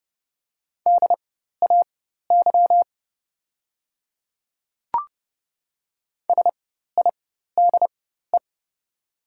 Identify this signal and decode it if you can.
Morse code, heard as a keyed tone, about 30 wpm, spaced as for 14 wpm, 700 Hz. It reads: DAY SIDE
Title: 4x Character Spacing - Top 100 Words - 30wpm